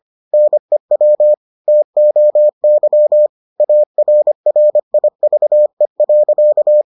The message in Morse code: NEW TOY ARRIVE.

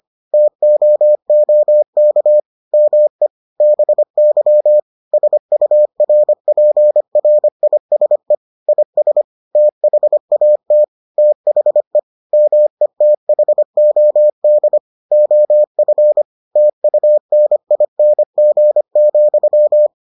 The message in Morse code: TOOK ME BY SURPRISE IS THAT THE METHOD OF TUNING,

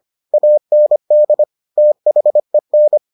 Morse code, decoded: AND THEN

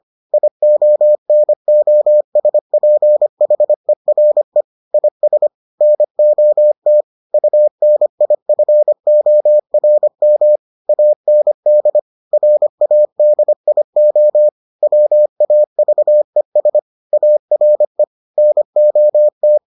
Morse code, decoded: IONOSPHERE IS NOT UNIFORM AND RADIO WAVES ARE NOT

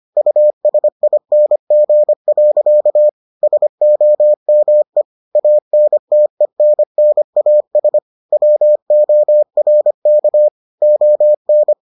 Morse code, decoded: USING. SOME ANTENNAS WORK ON

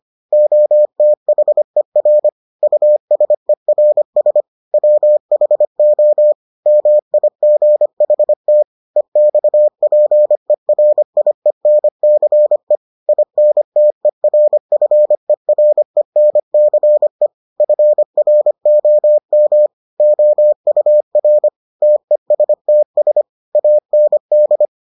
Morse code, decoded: OTHER USERS WHO MIGHT EXPERIENCE INTERFERENCE FROM OUR TESTS AND

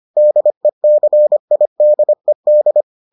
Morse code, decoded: DECIDED

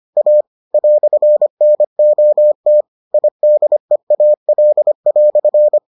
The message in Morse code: A "NOT IDEAL"

